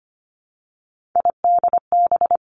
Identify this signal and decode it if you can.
Morse code, heard as a keyed tone, about 25 wpm, 700 Hz. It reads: IDB
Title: Three Challenging Letter-Number Combinations 25wpm